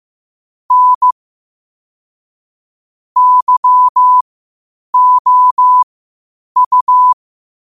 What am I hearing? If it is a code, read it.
Morse code, heard as a keyed tone, about 15 wpm, spaced as for 8 wpm, 1000 Hz. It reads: N YOU
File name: 8wpm